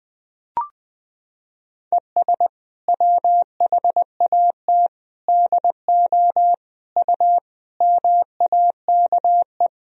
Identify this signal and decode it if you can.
Morse code, heard as a keyed tone, about 20 wpm, 700 Hz. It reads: ES WHAT DO U MAKE
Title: Sentences from Top 100 Words and Abbreviations 20wpm